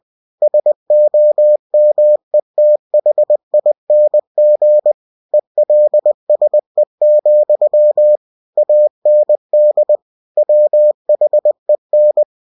Morse code, heard as a keyed tone, about 20 wpm, 600 Hz. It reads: SOMETHING ELSE, AND WHEN
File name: cw-books / winnie-the-pooh / winnie-20wpm-page-0021